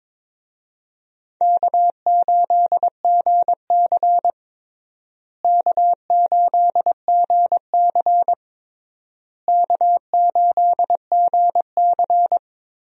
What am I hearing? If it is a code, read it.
Morse code, heard as a keyed tone, about 22 wpm, 700 Hz. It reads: K8GC K8GC K8GC
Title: USA Call Signs - 4 Characters - Repeated 3x - 22wpm